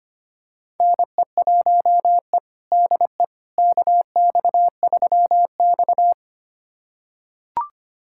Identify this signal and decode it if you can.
Morse code, heard as a keyed tone, about 25 wpm, 700 Hz. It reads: NE1E DE KX3X
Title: USA Callsign DE Callsign - 4 Characters - 25wpm